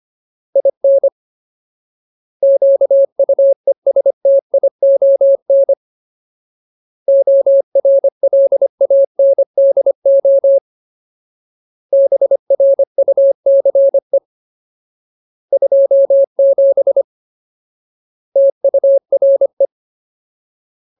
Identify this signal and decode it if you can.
Morse code, heard as a keyed tone, about 25 wpm, 550 Hz. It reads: IN QUESTION ORLANDO BRUCE 27 TURE
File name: Ultimate Word List / 25WPM Ultimate